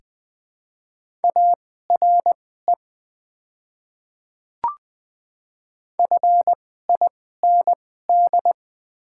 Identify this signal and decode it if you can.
Morse code, heard as a keyed tone, about 20 wpm, spaced as for 15 wpm, 700 Hz. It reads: ARE FIND
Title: 2x Character Spacing - Top 200 Words - 20wpm